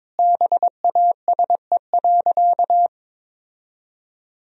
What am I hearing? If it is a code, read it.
Morse code, heard as a keyed tone, about 21 wpm, 700 Hz. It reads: BASE.
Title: Sentences from Top 1000 Words - Review Part 1 - 22wpm